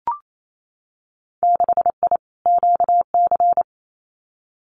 Morse code, heard as a keyed tone, about 28 wpm, 700 Hz. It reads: 6I QC
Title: Winter Field Day Exchanges Encore 28wpm